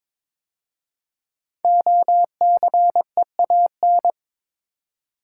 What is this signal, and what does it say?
Morse code, heard as a keyed tone, about 22 wpm, 700 Hz. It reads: OCEAN
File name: 5 Letter Words - First 3 Letters Given - 22wpm